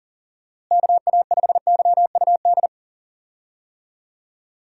Morse code, heard as a keyed tone, about 40 wpm, 700 Hz. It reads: KA5YUD
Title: USA Call Signs - 6 Characters - Repeated 2x - 40wpm